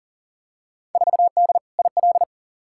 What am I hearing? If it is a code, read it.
Morse code, heard as a keyed tone, about 40 wpm, 700 Hz. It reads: 4D IL
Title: Field Day Exchanges Encore - 40wpm